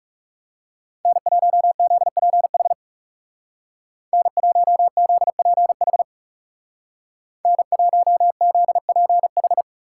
Morse code, heard as a keyed tone, about 39 wpm, 700 Hz. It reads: N1ZPH N1ZPH N1ZPH